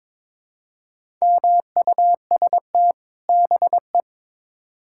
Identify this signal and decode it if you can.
Morse code, heard as a keyed tone, about 21 wpm, 700 Hz. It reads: MUST BE
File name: Build A Phrase - 2-Word Phrases - 22wpm